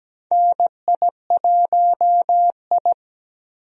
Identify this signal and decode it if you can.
Morse code, heard as a keyed tone, about 17 wpm, 700 Hz. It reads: NI1I
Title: USA Call Signs - 4 Characters - 17wpm